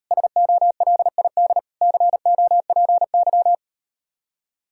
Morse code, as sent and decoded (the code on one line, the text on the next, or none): ... --- .-.. .. -.. / -.-. --- .--. -.--
SOLID COPY